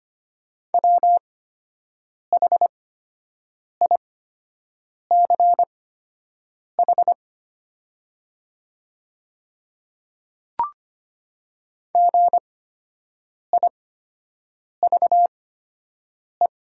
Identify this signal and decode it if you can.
Morse code, heard as a keyed tone, about 25 wpm, spaced as for 7 wpm, 700 Hz. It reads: WHICH GIVE
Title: 8x Character Spacing - Top 300 Words - 25wpm